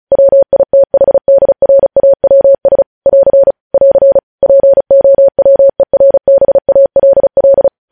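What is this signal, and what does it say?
Morse code, heard as a keyed tone, about 36 wpm, 550 Hz. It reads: WITHDRAWS <AR> <AR> POWERBALL
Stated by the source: Morse code news headlines sent at 35, beginning with vvv.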